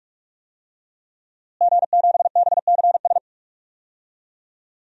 Morse code, heard as a keyed tone, about 46 wpm, 700 Hz. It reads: G7BCS
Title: DXCC-100 Call Signs - 5 Characters - Repeated 3x - 45wpm